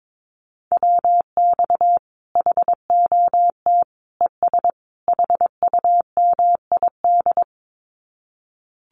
Morse code, heard as a keyed tone, about 22 wpm, 700 Hz. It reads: WX HOT ES HUMID
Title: Common QSO Phrases 22wpm